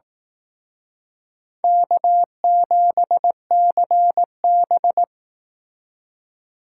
Morse code, heard as a keyed tone, about 18 wpm, 700 Hz. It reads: K7CB
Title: Speed Racing - USA 1x2 Call Signs 15wpm